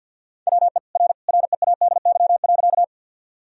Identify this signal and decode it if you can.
Morse code, heard as a keyed tone, about 46 wpm, 700 Hz. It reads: WE R READY.
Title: Sentences from Top 400 Words and Abbreviations 50wpm